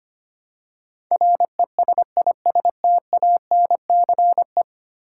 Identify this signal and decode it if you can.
Morse code, heard as a keyed tone, about 25 wpm, 690 Hz. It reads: RESISTANCE
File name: 10 Letter Words - First 3 Letters Given - 25wpm